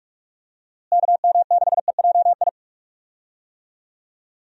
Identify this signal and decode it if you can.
Morse code, heard as a keyed tone, about 46 wpm, 700 Hz. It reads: KM6EJI
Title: USA Call Signs - 6 Characters - Repeated 3x - 45wpm